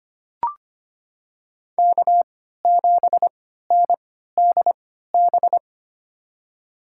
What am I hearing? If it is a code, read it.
Morse code, heard as a keyed tone, about 25 wpm, spaced as for 14 wpm, 700 Hz. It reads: K7NDB
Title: USA Call Signs - 5 Characters - 3x Character Spacing - 25wpm